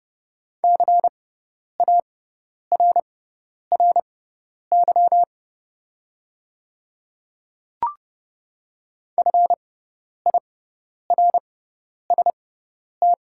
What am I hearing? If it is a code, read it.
Morse code, heard as a keyed tone, about 30 wpm, spaced as for 10 wpm, 700 Hz. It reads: CARRY FIRST
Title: 6x Character Spacing - Top 300 Words - 30wpm